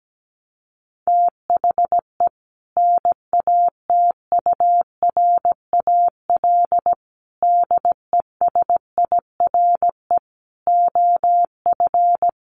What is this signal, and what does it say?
Morse code, heard as a keyed tone, about 17 wpm, 700 Hz. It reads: THE NATURAL DESIRE OF
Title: Sentences from Top 1100 Words 17wpm